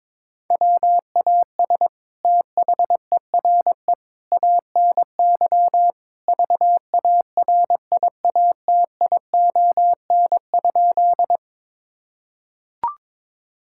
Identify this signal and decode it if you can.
Morse code, heard as a keyed tone, about 21 wpm, 700 Hz. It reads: WAS THERE ANY VARIATION?
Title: Sentences from Top 1600 Words 22wpm